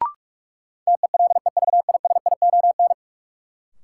Morse code, heard as a keyed tone, about 46 wpm, 700 Hz. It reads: TELEVISION
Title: Sentences from Top 1000 Words - Review Part 4 - New Word Drill - 45wpm